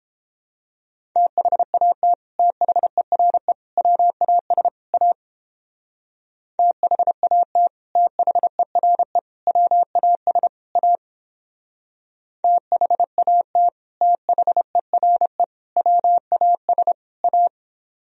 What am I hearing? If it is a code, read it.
Morse code, heard as a keyed tone, about 31 wpm, 700 Hz. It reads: THAT THERE WAS A THAT THERE WAS A THAT THERE WAS A